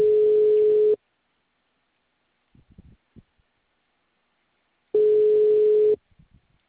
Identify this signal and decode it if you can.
Ringback tone (Europe, 425 Hz).